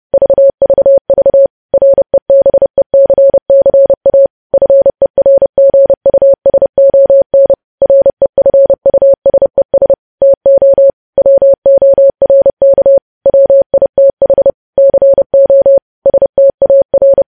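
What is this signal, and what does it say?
Morse code, heard as a keyed tone, about 30 wpm, 550 Hz. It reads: VVV REBECCA FERGUSON REFUSES TO WORK WITH CO STAR
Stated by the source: News headlines sent in Morse at 30 wpm, preceded by VVV.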